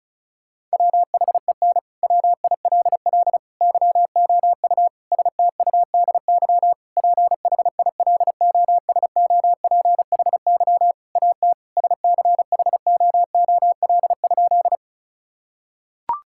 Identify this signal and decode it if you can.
Morse code, heard as a keyed tone, about 36 wpm, 700 Hz. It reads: WHEN WILL YOU STUDY PHILOSOPHY AT SCHOOL?
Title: Sentences from Top 1000 Words - Review Part 6 - 35wpm